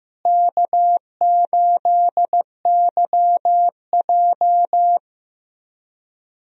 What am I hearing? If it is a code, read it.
Morse code, heard as a keyed tone, about 15 wpm, 700 Hz. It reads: K8YJ
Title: USA Call Signs - 4 Characters - Repeated 3x - 15wpm